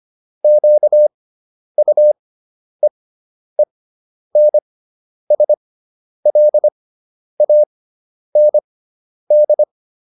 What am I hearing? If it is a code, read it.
Morse code, heard as a keyed tone, about 25 wpm, spaced as for 10 wpm, 600 Hz. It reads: QUEENSLAND
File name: QSO407 10